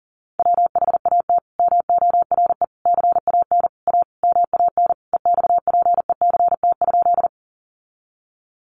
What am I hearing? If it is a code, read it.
Morse code, heard as a keyed tone, about 40 wpm, 700 Hz. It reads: WHAT MORE CAN A MAN EXPECT?
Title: Sentences from Top 1100 Words 40wpm